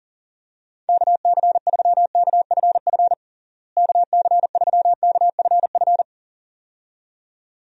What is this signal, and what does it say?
Morse code, heard as a keyed tone, about 40 wpm, 700 Hz. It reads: KC3KFF KC3KFF
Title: USA Call Signs - 6 Characters - Repeated 2x - 40wpm